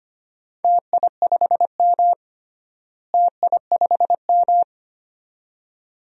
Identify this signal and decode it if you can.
Morse code, heard as a keyed tone, about 25 wpm, 700 Hz. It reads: TI5M TI5M
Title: DXCC-100 Call Signs - 4 Characters - Repeated 2x - 25wpm